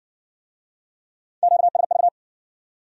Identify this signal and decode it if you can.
Morse code, heard as a keyed tone, about 56 wpm, 700 Hz. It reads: QSV